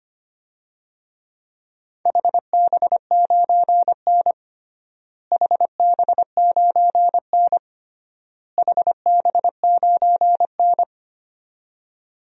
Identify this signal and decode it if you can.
Morse code, heard as a keyed tone, about 25 wpm, 700 Hz. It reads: HB9N HB9N HB9N
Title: DXCC-100 Call Signs - 4 Characters - Repeated 3x - 25wpm